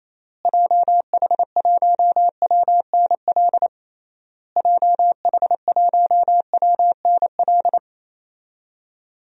Morse code, heard as a keyed tone, about 28 wpm, 700 Hz. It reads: JH1WNL JH1WNL
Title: DXCC-100 Call Signs - 6 Characters - Repeated 3x - 28wpm